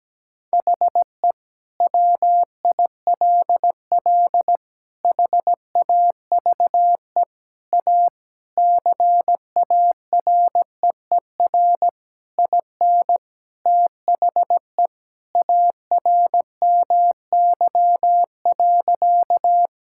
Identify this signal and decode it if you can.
Morse code, heard as a keyed tone, about 17 wpm, 700 Hz. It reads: HE WILL HAVE A CAREER IN THE ARMY.
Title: Sentences from Top 1000 Words - Review Part 6 - 17wpm